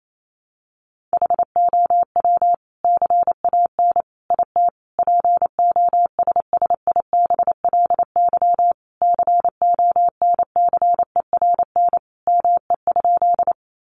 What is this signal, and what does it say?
Morse code, heard as a keyed tone, about 28 wpm, 700 Hz. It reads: HOW CAN IT POSSIBLY CONCERN ME?
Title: Sentences from Top 1300 Words 28wpm